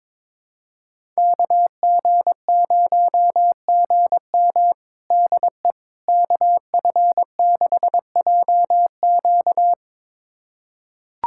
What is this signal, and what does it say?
Morse code, heard as a keyed tone, about 22 wpm, 700 Hz. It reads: KG0GM DE KF6JQ
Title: USA Callsign DE Callsign - 5 Characters - 22wpm